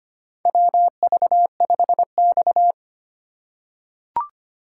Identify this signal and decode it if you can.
Morse code, heard as a keyed tone, about 25 wpm, 700 Hz. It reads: WV5X
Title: USA Call Signs - 4 Characters - 2x Character Spacing - 25wpm